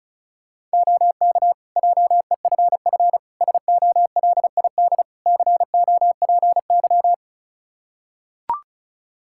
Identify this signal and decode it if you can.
Morse code, heard as a keyed tone, about 34 wpm, 700 Hz. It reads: OK JEFF SOLID COPY
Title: K1USN SST QSO Practice 35wpm